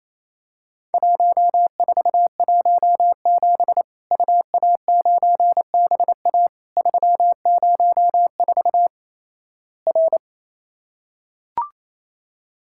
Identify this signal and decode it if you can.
Morse code, heard as a keyed tone, about 28 wpm, 700 Hz. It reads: 1417 UA9BA 304 R
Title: Worked All Europe DX Contest - QTC Exchange - 28wpm